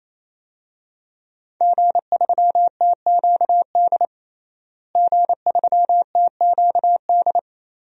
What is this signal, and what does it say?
Morse code, heard as a keyed tone, about 28 wpm, 700 Hz. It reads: G3TQD G3TQD
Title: DXCC-100 Call Signs - 5 Characters - Repeated 3x - 28wpm